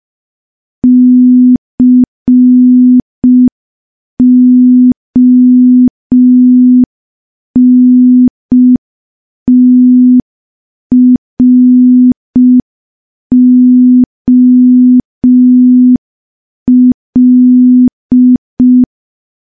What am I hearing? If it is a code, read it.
Morse code, heard as a keyed tone, about 5 wpm, 250 Hz. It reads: CONTROL
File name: morse / morse-training-05WPM